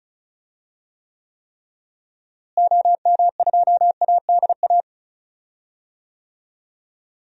Morse code, heard as a keyed tone, about 34 wpm, 700 Hz. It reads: OM2ADA
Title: DXCC-100 Call Signs - 6 Characters - Repeated 3x - 35wpm